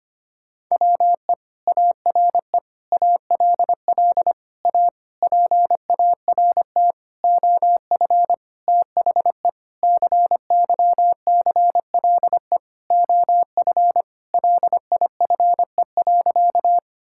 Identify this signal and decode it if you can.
Morse code, heard as a keyed tone, about 25 wpm, 700 Hz. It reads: WE ARE ALL A PART OF THE CYCLE OF LIFE.
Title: Sentences from Top 1000 Words - Review Part 6 - 25wpm